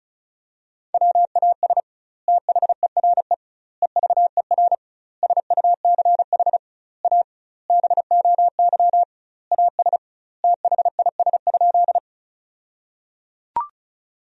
Morse code, heard as a keyed tone, about 36 wpm, 700 Hz. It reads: WAS THERE EVER SUCH A BOY AS THIS?
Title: 2x Word-Spacing - Sentences from Top 400 Words - 35wpm